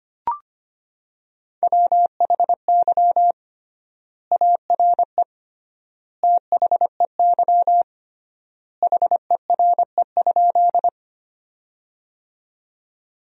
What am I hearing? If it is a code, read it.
Morse code, heard as a keyed tone, about 25 wpm, 700 Hz. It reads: WHY ARE THEY HERE?
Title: 3x Word-Spacing - Sentences from Top 200 Words - 25wpm